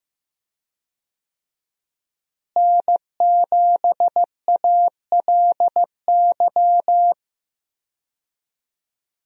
Morse code, heard as a keyed tone, about 15 wpm, 700 Hz. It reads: N7ALY
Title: USA Call Signs - 5 Characters - Repeated 2x - 15wpm